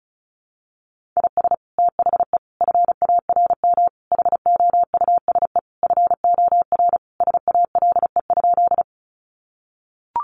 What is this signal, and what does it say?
Morse code, heard as a keyed tone, about 34 wpm, 700 Hz. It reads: IS THE FARM HOUSE FOR SALE?
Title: Sentences from Top 1400 Words 35wpm